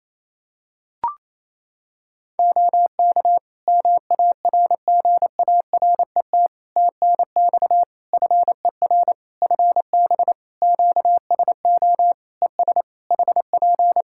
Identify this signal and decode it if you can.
Morse code, heard as a keyed tone, about 27 wpm, 700 Hz. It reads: OK MARGARET TNX FER FB QSO ES HP
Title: Practice QSO - Exchange 4 - Ending - 28wpm